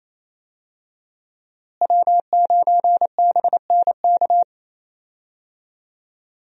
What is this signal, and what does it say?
Morse code, heard as a keyed tone, about 28 wpm, 700 Hz. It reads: W9BNK
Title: USA Call Signs - 5 Characters - Repeated 2x - 28wpm